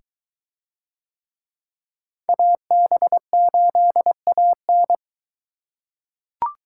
Morse code, heard as a keyed tone, about 20 wpm, 700 Hz. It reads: AB8AN